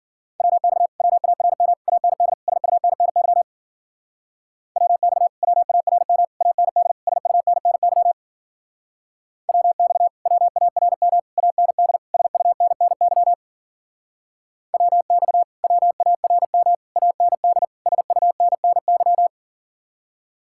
Morse code, heard as a keyed tone, about 52 wpm, 700 Hz. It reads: WX WARM AND SUNNY WX WARM AND SUNNY WX WARM AND SUNNY WX WARM AND SUNNY